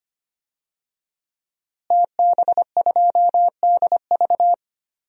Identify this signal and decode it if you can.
Morse code, heard as a keyed tone, about 25 wpm, 700 Hz. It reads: TB2DV
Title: DXCC-100 Call Signs - 5 Characters - Repeated 2x - 25wpm